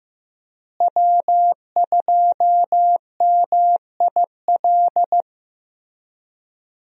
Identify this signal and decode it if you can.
Morse code, heard as a keyed tone, about 15 wpm, 700 Hz. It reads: W2MIL
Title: USA Call Signs - 5 Characters - Repeated 2x - 15wpm